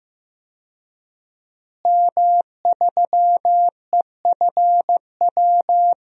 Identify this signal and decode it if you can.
Morse code, heard as a keyed tone, about 15 wpm, 700 Hz. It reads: M3EFW
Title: DXCC-100 Call Signs - 5 Characters - Repeated 2x - 15wpm